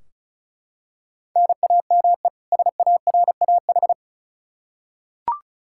Morse code, heard as a keyed tone, about 34 wpm, 700 Hz. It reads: NAME SARAH
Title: K1USN SST QSO Practice 35wpm